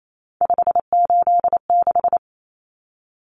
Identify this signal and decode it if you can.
Morse code, heard as a keyed tone, about 28 wpm, 700 Hz. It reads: 586
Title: Common Cut Numbers - 3-Digit 28wpm